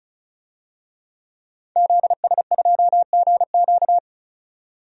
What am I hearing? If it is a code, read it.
Morse code, heard as a keyed tone, about 36 wpm, 700 Hz. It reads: ZS2GQ
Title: DXCC-100 Call Signs - 5 Characters - 35wpm